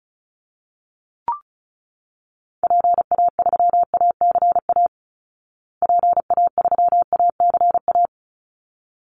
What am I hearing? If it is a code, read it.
Morse code, heard as a keyed tone, about 34 wpm, 700 Hz. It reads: PA3ACA PA3ACA
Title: Call Signs Repeated 3x - 35wpm